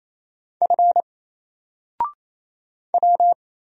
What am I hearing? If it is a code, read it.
Morse code, heard as a keyed tone, about 27 wpm, 700 Hz. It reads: F W
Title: Single Character (Letter:Number:Punctuation:Prosign) 28wpm